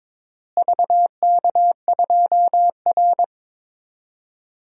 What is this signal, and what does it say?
Morse code, heard as a keyed tone, about 22 wpm, 700 Hz. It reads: VK2R
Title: DXCC-100 Call Signs - 4 Characters - Repeated 3x - 22wpm